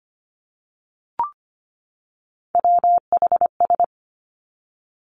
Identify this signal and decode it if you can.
Morse code, heard as a keyed tone, about 25 wpm, 700 Hz. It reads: WHS
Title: Three Challenging Letter-Number Combinations 25wpm